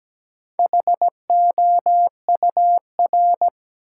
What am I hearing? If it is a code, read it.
Morse code, heard as a keyed tone, about 17 wpm, 700 Hz. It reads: HOUR
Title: Sentences from Top 400 Words - New Word Drill - 17wpm